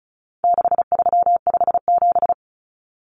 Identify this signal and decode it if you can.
Morse code, heard as a keyed tone, about 34 wpm, 700 Hz. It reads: 6357
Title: Common Cut Numbers - 4-Digit 35wpm